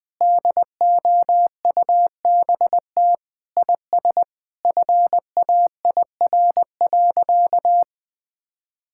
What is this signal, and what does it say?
Morse code, heard as a keyed tone, about 21 wpm, 700 Hz. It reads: DOUBT IS FAIR.
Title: Sentences with 8-Word Phrases - 20wpm